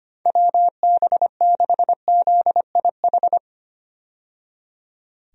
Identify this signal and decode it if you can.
Morse code, heard as a keyed tone, about 25 wpm, 700 Hz. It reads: WB6ZIH
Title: USA Call Signs - 6 Characters - 25wpm